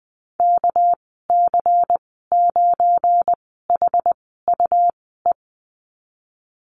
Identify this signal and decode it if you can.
Morse code, heard as a keyed tone, about 19 wpm, 700 Hz. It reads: K C 9 H U E
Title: 200 Lesson 32g - Letter k - Call Signs 20wpm